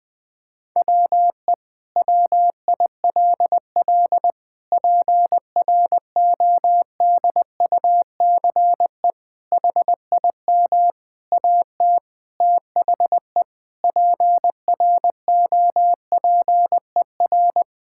Morse code, heard as a keyed tone, about 20 wpm, 700 Hz. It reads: WE WILL PRODUCE HIM AT THE PROPER
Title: Sentences from Top 1000 Words - Review Part 5 - 20wpm